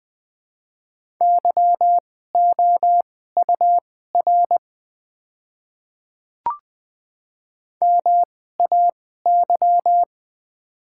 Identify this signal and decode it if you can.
Morse code, heard as a keyed tone, about 20 wpm, spaced as for 15 wpm, 700 Hz. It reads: YOUR MAY
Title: 2x Character Spacing - Top 200 Words - 20wpm